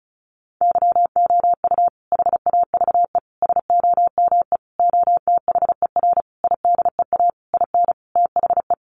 Morse code, heard as a keyed tone, about 34 wpm, 700 Hz. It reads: YOU HAVE SOME OTHER IDEA IN THE
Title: Sentences from Top 1300 Words 35wpm